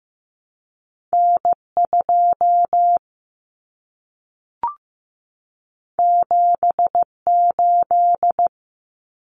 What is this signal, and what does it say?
Morse code, heard as a keyed tone, about 15 wpm, 700 Hz. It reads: N2 78
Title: Common Cut Numbers - 2-Digit 15wpm